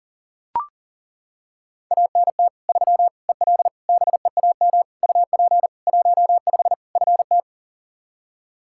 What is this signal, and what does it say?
Morse code, heard as a keyed tone, about 40 wpm, 700 Hz. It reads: ANT 3 EL BEAM UP 15 FT
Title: K1USN SST QSO Practice 40wpm